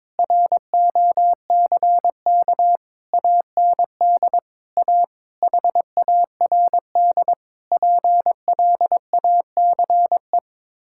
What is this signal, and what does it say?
Morse code, heard as a keyed tone, about 22 wpm, 700 Hz. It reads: ROCK AND A HARD PLACE
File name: practice / 22wpm / Sets of 8 Words - 22wpm